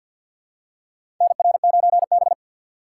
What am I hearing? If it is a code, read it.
Morse code, heard as a keyed tone, about 53 wpm, 700 Hz. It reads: NR9B
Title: USA Call Signs - 3x Character Spacing - 50wpm